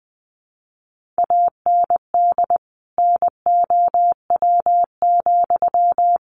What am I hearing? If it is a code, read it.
Morse code, heard as a keyed tone, about 19 wpm, 700 Hz. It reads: AND NOW,
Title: Sentences from Top 1500 Words 20wpm